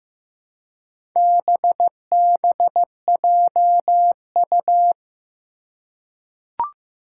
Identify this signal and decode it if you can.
Morse code, heard as a keyed tone, about 15 wpm, 700 Hz. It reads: BBJU
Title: Four Challenging Letter-Number Combinations 15wpm